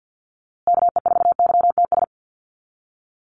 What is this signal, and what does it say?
Morse code, heard as a keyed tone, about 50 wpm, 700 Hz. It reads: KE4YTS
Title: USA Call Signs Repeated 3x 50wpm